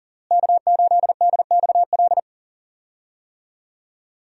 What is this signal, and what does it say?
Morse code, heard as a keyed tone, about 40 wpm, 690 Hz. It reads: K8DXL
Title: USA Call Signs - 5 Characters - Repeated 3x - 40wpm